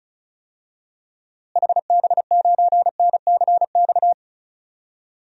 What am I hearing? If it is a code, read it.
Morse code, heard as a keyed tone, about 35 wpm, 700 Hz. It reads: HB9NCX
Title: DXCC-100 Call Signs - 6 Characters - Repeated 2x - 35wpm